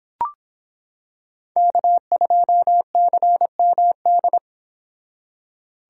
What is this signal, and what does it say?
Morse code, heard as a keyed tone, about 26 wpm, 700 Hz. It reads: K2CMD